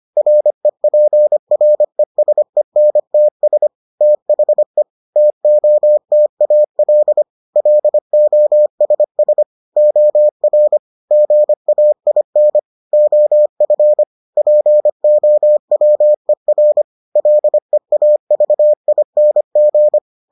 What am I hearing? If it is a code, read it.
Morse code, heard as a keyed tone, about 25 wpm, 600 Hz. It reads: REPRESENTS THE TOTAL LOSS OR GAIN OF POWER LEAVING